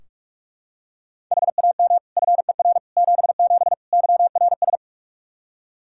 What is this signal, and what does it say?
Morse code, heard as a keyed tone, about 43 wpm, 700 Hz. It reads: HAM FER 77 YRS
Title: K1USN SST QSO Practice 45wpm